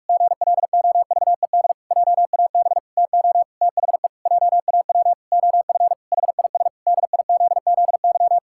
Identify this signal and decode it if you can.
Morse code, heard as a keyed tone, about 46 wpm, 700 Hz. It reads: GLOVED JAB TO THE JAW OF HIS DIZZY
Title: Pangram Challenge v2 - 45wpm